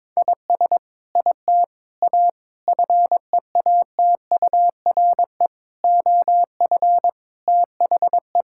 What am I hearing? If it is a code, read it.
Morse code, heard as a keyed tone, about 22 wpm, 700 Hz. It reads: IS IT A FEATURE OF THE
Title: Sentences from Top 1000 Words - Review Part 5 - 22wpm